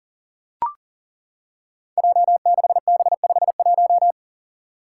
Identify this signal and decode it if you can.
Morse code, heard as a keyed tone, about 42 wpm, 700 Hz. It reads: J6B51
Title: Five Challenging Letter-Number Combinations 40wpm